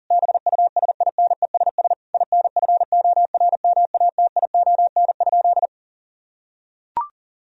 Morse code, heard as a keyed tone, about 40 wpm, 700 Hz. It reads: BUSINESS INFORMATION?
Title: Sentences from Top 1000 Words - Review Part 3 - 40wpm